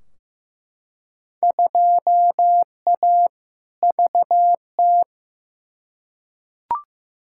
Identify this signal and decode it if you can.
Morse code, heard as a keyed tone, about 15 wpm, 700 Hz. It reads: 2A VT
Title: Field Day Exchanges Encore - 15wpm